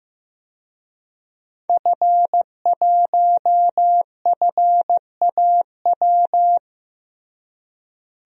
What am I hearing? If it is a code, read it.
Morse code, heard as a keyed tone, about 16 wpm, 700 Hz. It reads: F1FAW
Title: DXCC-100 Call Signs - 5 Characters - Repeated 3x - 15wpm